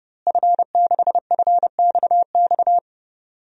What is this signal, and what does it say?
Morse code, heard as a keyed tone, about 30 wpm, 700 Hz. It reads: F6FXX
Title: DXCC-100 Call Signs - 5 Characters - Repeated 3x - 30wpm